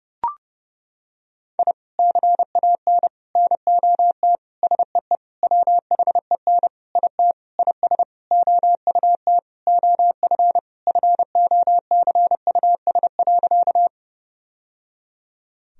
Morse code, heard as a keyed tone, about 32 wpm, 700 Hz. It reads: I CAN NOT SEE WHEN IT IS OUT OF FOCUS.
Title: Sentences from Top 1000 Words - Review Part 4 - 30wpm